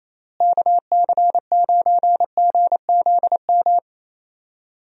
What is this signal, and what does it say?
Morse code, heard as a keyed tone, about 27 wpm, 700 Hz. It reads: KC9GZM
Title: USA Call Signs - 6 Characters - Repeated 3x - 28wpm